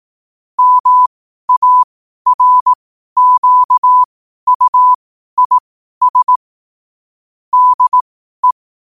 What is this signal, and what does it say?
Morse code, heard as a keyed tone, about 18 wpm, spaced as for 12 wpm, 1000 Hz. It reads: MARQUIS DE
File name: qotdfiles / 12wpm